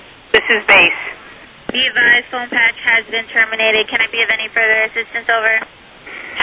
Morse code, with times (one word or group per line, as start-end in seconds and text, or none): none